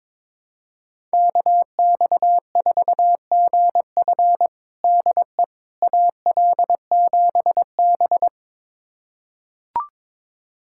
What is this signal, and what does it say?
Morse code, heard as a keyed tone, about 22 wpm, 700 Hz. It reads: KX4GF DE AL7B
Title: USA Callsign DE Callsign 22wpm